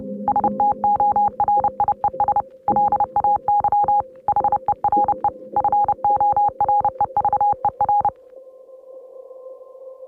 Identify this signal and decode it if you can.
Morse code, heard as a keyed tone, about 29 wpm, 820 Hz. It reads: STORIES LAY HERE FOREVER